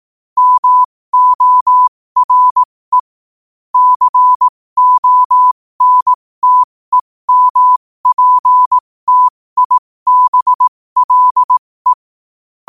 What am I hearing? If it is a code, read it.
Morse code, heard as a keyed tone, about 18 wpm, spaced as for 15 wpm, 1000 Hz. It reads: MORE CONTEMPTIBLE
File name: qotdfiles / 15wpm